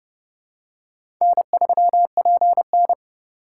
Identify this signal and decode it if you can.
Morse code, heard as a keyed tone, about 30 wpm, 700 Hz. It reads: N3PN
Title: USA Call Signs - 4 Characters - 4x Character Spacing - 30wpm